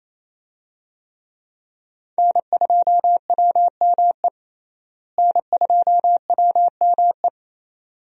Morse code, heard as a keyed tone, about 28 wpm, 700 Hz. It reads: N2WME N2WME
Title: USA Call Signs - 5 Characters - Repeated 2x - 28wpm